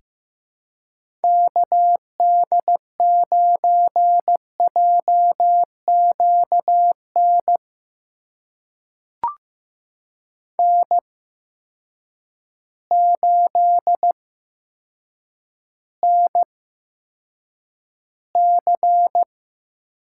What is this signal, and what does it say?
Morse code, heard as a keyed tone, about 15 wpm, spaced as for 7 wpm, 700 Hz. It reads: KD9JQN N8NC
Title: USA Call Signs - 8x Character Spacing - 15wpm